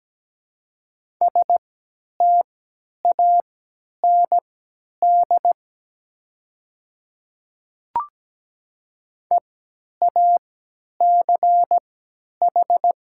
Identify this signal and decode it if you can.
Morse code, heard as a keyed tone, about 17 wpm, spaced as for 10 wpm, 700 Hz. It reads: STAND EACH
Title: 3x Character Spacing - Top 300 Words - 17wpm